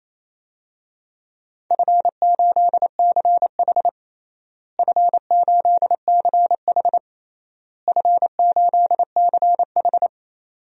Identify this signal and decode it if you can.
Morse code, heard as a keyed tone, about 28 wpm, 700 Hz. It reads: F8CH F8CH F8CH
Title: DXCC-100 Call Signs - 4 Characters - Repeated 3x - 28wpm